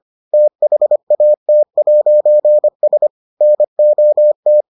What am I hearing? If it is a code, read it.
Morse code, heard as a keyed tone, about 25 wpm, 600 Hz. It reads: THAT'S NOT